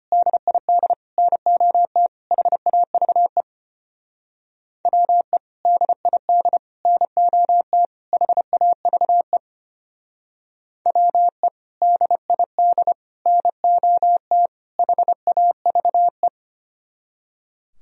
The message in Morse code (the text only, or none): DID NOT HAVE WE DID NOT HAVE WE DID NOT HAVE